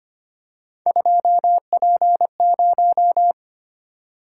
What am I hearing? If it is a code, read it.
Morse code, heard as a keyed tone, about 25 wpm, 700 Hz. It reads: 2P0
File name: Three Letter-Number Combinations - 25wpm